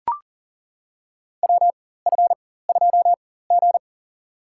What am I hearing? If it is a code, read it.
Morse code, heard as a keyed tone, about 40 wpm, spaced as for 19 wpm, 700 Hz. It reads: WF2G
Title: USA Call Signs - 4 Characters - 4x Character Spacing - 40wpm